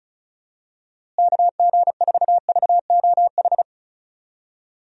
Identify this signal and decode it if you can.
Morse code, heard as a keyed tone, about 34 wpm, 700 Hz. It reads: KG4VOH
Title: USA Call Signs - 6 Characters - Repeated 3x - 35wpm